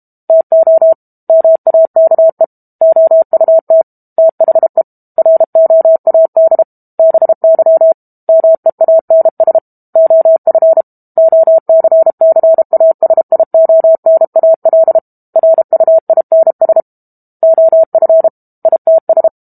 Morse code, heard as a keyed tone, about 32 wpm, 650 Hz. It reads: TO MAKE OUT THE ROAD BY MEANS OF OCCASIONAL RUINS OF ITS
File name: Book Two / 06-Chapter 6 - 32 WPM